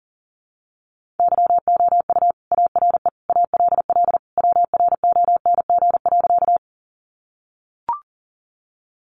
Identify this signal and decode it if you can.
Morse code, heard as a keyed tone, about 40 wpm, 700 Hz. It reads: YOU ARE ALL WRONG.
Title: Sentences from Top 900 Words 40wpm